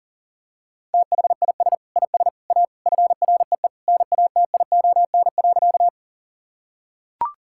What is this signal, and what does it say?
Morse code, heard as a keyed tone, about 40 wpm, 700 Hz. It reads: THIS IS A FREE NATION.
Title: Sentences from Top 1000 Words - Review Part 6 - 40wpm